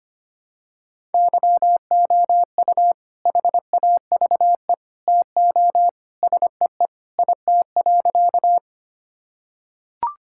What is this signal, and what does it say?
Morse code, heard as a keyed tone, about 25 wpm, 700 Hz. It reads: YOU HAVE TO SEE IT.
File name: practice / 25wpm / Sentences with 2-Word Phrases - 25wpm